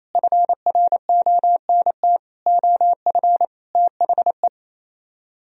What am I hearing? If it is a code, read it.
Morse code, heard as a keyed tone, about 37 wpm, 700 Hz. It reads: FRONT OF THE